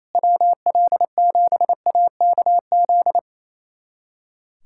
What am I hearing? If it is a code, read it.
Morse code, heard as a keyed tone, about 28 wpm, 700 Hz. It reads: WL7AKZ
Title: USA Callsign DE Callsign - 6 Characters - 28wpm